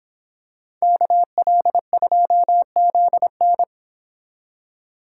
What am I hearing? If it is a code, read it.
Morse code, heard as a keyed tone, about 26 wpm, 700 Hz. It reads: KL2ZN